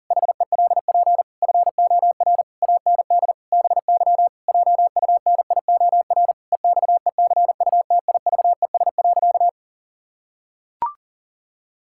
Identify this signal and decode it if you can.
Morse code, heard as a keyed tone, about 37 wpm, 700 Hz. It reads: HELP FOR AND BY JUNIOR EXECUTIVES.
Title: Pangram Challenge v2 - 40wpm